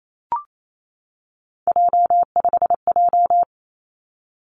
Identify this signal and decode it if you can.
Morse code, heard as a keyed tone, about 28 wpm, 700 Hz. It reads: J5J
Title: Three Challenging Letter-Number Combinations 28wpm